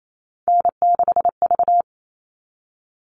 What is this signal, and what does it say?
Morse code, heard as a keyed tone, about 28 wpm, 700 Hz. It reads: N6V
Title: All Cut Numbers - 3-Digit 28wpm